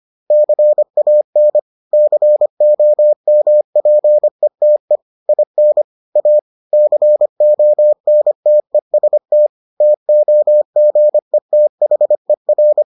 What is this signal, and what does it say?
Morse code, heard as a keyed tone, about 25 wpm, 600 Hz. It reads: CAN COMPETE IN A CONTEST TOGETHER